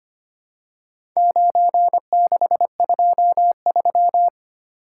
Morse code, heard as a keyed tone, about 25 wpm, 700 Hz. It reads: 9623
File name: practice / 25wpm / Four Letter-Number Combinations - 25wpm